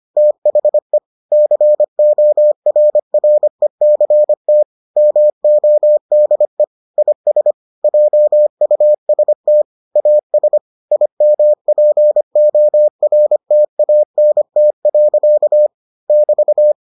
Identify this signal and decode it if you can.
Morse code, heard as a keyed tone, about 25 wpm, 600 Hz. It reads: THE CORRECT MODE IS JUST AS IMPORTANT. <BT>